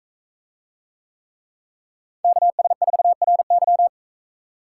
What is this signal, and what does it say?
Morse code, heard as a keyed tone, about 42 wpm, 700 Hz. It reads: KS4RY